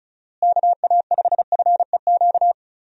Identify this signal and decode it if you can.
Morse code, heard as a keyed tone, about 34 wpm, 700 Hz. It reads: KA5FEQ
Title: USA Call Signs - 6 Characters - Repeated 3x - 35wpm